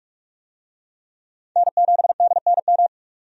Morse code, heard as a keyed tone, about 45 wpm, 700 Hz. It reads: N7DNM
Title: USA Call Signs - 5 Characters - 45wpm